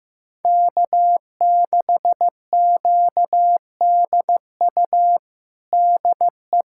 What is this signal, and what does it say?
Morse code, heard as a keyed tone, about 15 wpm, 700 Hz. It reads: K6QDU DE
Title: USA Callsign DE Callsign 15wpm